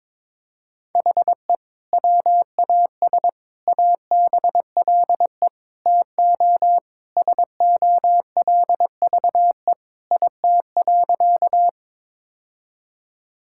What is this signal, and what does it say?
Morse code, heard as a keyed tone, about 22 wpm, 700 Hz. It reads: HE WAS ABLE TO SOLVE IT.
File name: practice / 22wpm / Sentences with 3-Word Phrases - 22wpm